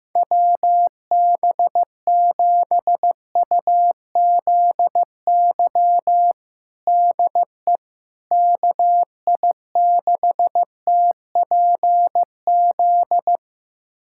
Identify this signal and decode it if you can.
Morse code, heard as a keyed tone, about 15 wpm, 700 Hz. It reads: WB7UZY DE KI6TPZ
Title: USA Callsign DE Callsign - 6 Characters - 15wpm